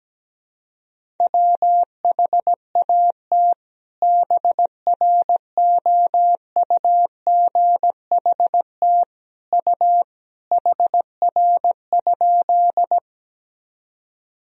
Morse code, heard as a keyed tone, about 17 wpm, 700 Hz. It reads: WHAT BROUGHT U HR?
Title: Sentences from Top 500 Words and Abbreviations 17wpm